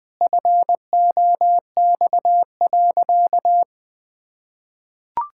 Morse code, heard as a keyed tone, about 21 wpm, 700 Hz. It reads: FOX.
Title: Pangram Challenge v2 - 20wpm